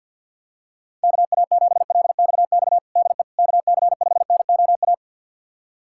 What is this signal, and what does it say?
Morse code, heard as a keyed tone, about 51 wpm, 700 Hz. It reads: KA7LXX DE KC5NOA
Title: USA Callsign DE Callsign 50wpm